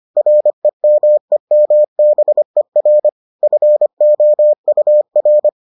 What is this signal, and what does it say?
Morse code, heard as a keyed tone, about 25 wpm, 600 Hz. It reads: REMEMBER FOUR